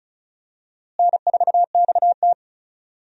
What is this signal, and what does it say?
Morse code, heard as a keyed tone, about 34 wpm, 700 Hz. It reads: N4XT
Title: USA Call Signs - 4 Characters - Repeated 2x - 35wpm